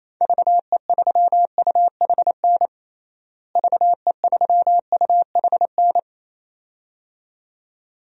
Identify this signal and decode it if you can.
Morse code, heard as a keyed tone, about 28 wpm, 700 Hz. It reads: VE3UHN VE3UHN
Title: DXCC-100 Call Signs - 6 Characters - Repeated 2x - 28wpm